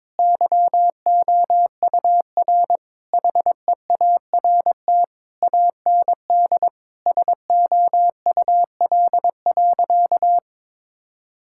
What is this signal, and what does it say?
Morse code, heard as a keyed tone, about 22 wpm, 700 Hz. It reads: YOUR HEART AND SOUL.
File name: Sentences with 9-Word Phrases - 22wpm